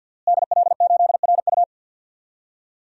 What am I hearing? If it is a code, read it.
Morse code, heard as a keyed tone, about 53 wpm, 700 Hz. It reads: DL8RU
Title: DXCC-100 Call Signs - 5 Characters - Repeated 2x - 50wpm